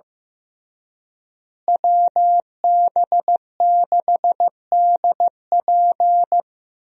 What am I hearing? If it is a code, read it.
Morse code, heard as a keyed tone, about 16 wpm, 700 Hz. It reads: WB6DP
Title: USA Call Signs - 5 Characters - Repeated 3x - 15wpm